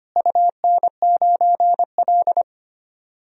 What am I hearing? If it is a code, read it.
Morse code, heard as a keyed tone, about 25 wpm, 700 Hz. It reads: UN9L
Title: DXCC-100 Call Signs - 4 Characters - Repeated 3x - 25wpm